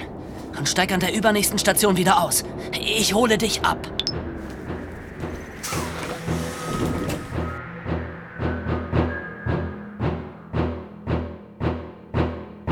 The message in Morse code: none